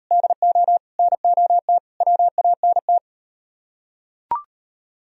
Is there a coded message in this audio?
Yes: Morse code from 0.1 to 3.0 s, heard as a keyed tone, about 39 wpm, 700 Hz. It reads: DO NOT WANT